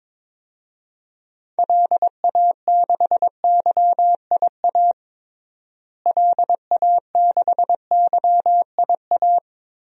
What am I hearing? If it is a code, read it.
Morse code, heard as a keyed tone, about 22 wpm, 700 Hz. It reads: LA6YIA LA6YIA
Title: DXCC-100 Call Signs - 6 Characters - Repeated 3x - 22wpm